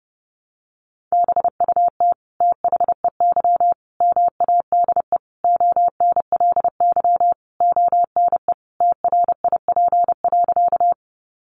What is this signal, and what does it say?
Morse code, heard as a keyed tone, about 30 wpm, 700 Hz. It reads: BUT THEY MADE ONLY ONE TRIP.
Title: Sentences from Top 1300 Words 30wpm